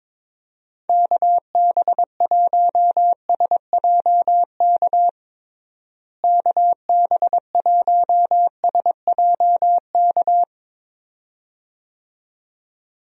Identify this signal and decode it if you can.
Morse code, heard as a keyed tone, about 22 wpm, 700 Hz. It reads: KB1SJK KB1SJK
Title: USA Call Signs - 6 Characters - Repeated 2x - 22wpm